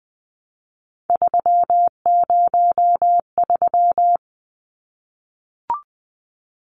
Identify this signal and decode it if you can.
Morse code, heard as a keyed tone, about 20 wpm, 700 Hz. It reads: 303
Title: Numbers - 3-Digit 20wpm